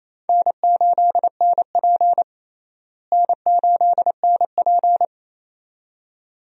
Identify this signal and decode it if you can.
Morse code, heard as a keyed tone, about 28 wpm, 700 Hz. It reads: N8NP N8NP
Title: USA Call Signs - 4 Characters - Repeated 2x - 28wpm